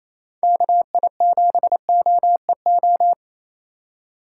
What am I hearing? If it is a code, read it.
Morse code, heard as a keyed tone, about 28 wpm, 700 Hz. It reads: KI7OEO
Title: USA Call Signs - 6 Characters - 2x Character Spacing - 28wpm